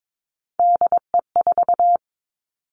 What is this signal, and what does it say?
Morse code, heard as a keyed tone, about 22 wpm, 700 Hz. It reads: DE4
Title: Three Challenging Letter-Number Combinations 22wpm